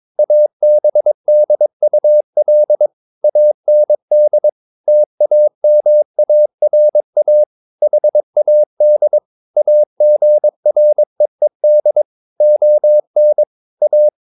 Morse code, heard as a keyed tone, about 22 wpm, 600 Hz. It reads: ABDUL AND TAMARA HAD AGREED ON A
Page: Romanen Never af Ken Follet på engelsk.